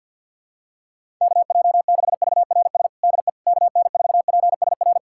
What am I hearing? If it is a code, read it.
Morse code, heard as a keyed tone, about 52 wpm, 700 Hz. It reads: KJ6VRS DE KN4PSR
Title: USA Callsign DE Callsign - 6 Characters - 50wpm